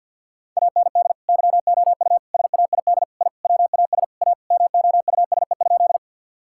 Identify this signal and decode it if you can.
Morse code, heard as a keyed tone, about 52 wpm, 700 Hz. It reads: AND YOU SAID I WAS A MOUSE?
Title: Sentences from Top 1000 Words - Review Part 5 - 50wpm